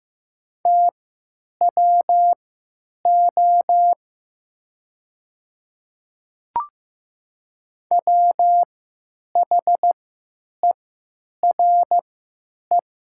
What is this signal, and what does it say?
Morse code, heard as a keyed tone, about 15 wpm, spaced as for 9 wpm, 700 Hz. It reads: TWO WHERE
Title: 3x Character Spacing - Top 200 Words - 15wpm